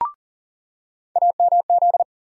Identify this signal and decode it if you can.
Morse code, heard as a keyed tone, about 40 wpm, 700 Hz. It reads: AMZ
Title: USA Call Sign Postfixes - 3 Letters 40wpm